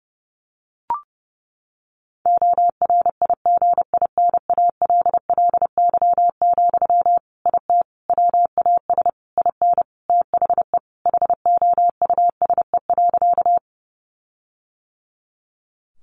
Morse code, heard as a keyed tone, about 30 wpm, 700 Hz. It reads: ORIGINALLY, IT WAS IN THE HOUSE.
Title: Sentences from Top 1500 Words 30wpm